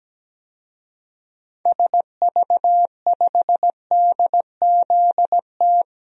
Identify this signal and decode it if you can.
Morse code, heard as a keyed tone, about 17 wpm, 700 Hz. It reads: SV5DZT
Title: DXCC-100 Call Signs - 6 Characters - 17wpm